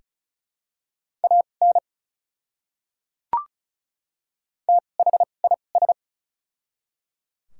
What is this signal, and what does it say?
Morse code, heard as a keyed tone, about 36 wpm, spaced as for 25 wpm, 700 Hz. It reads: AN THIS
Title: 2x Character Spacing - Top 100 Words - 35wpm